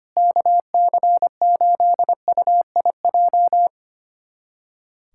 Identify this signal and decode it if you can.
Morse code, heard as a keyed tone, about 25 wpm, 700 Hz. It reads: KC8UIJ
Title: USA Callsign DE Callsign - 6 Characters - 25wpm